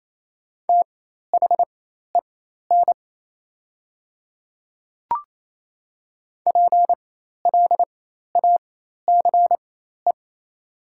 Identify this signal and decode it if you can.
Morse code, heard as a keyed tone, about 28 wpm, spaced as for 13 wpm, 690 Hz. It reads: THEN PLACE
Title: 4x Character Spacing - Top 200 Words - 28wpm